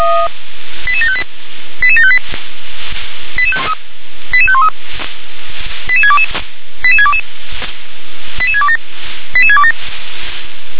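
Five-tone selective call (ZVEI), heard as quick runs of five tones, tones of about 70 ms. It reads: 88758 88758 88513 88622 88622 88627 88627
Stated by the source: ZVEI five-tone selective call, received on SDR.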